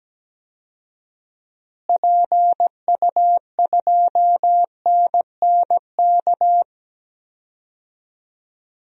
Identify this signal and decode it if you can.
Morse code, heard as a keyed tone, about 17 wpm, 700 Hz. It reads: PU2NNK
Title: DXCC-100 Call Signs - 6 Characters - Repeated 2x - 17wpm